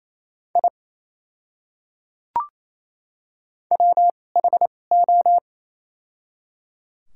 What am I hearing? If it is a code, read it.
Morse code, heard as a keyed tone, about 28 wpm, spaced as for 20 wpm, 700 Hz. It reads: I WHO
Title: 2x Character Spacing - Top 200 Words - 28wpm